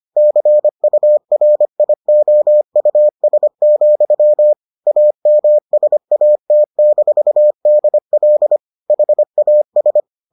Morse code, heard as a keyed tone, about 25 wpm, 600 Hz. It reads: CURIOUS, AMSAT-DL HAS